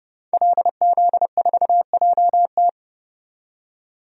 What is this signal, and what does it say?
Morse code, heard as a keyed tone, about 30 wpm, 700 Hz. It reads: LZ4JT
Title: DXCC-100 Call Signs - 5 Characters - 30wpm